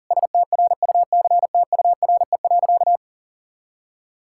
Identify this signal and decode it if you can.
Morse code, heard as a keyed tone, about 40 wpm, 700 Hz. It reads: STRUCTURE.
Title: Sentences from Top 1000 Words - Review Part 6 - 40wpm